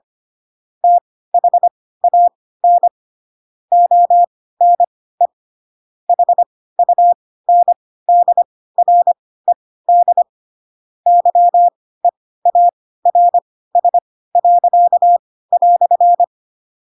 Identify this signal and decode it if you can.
Morse code, heard as a keyed tone, about 25 wpm, spaced as for 16 wpm, 700 Hz. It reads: THAN ONE HUNDRED YEARS."
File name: Pinocchio / Pinocchio ch4 10 25 ik1hge